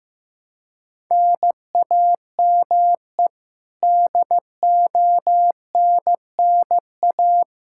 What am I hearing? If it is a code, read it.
Morse code, heard as a keyed tone, about 15 wpm, 700 Hz. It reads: NAME DONNA
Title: K1USN SST QSO Practice 15wpm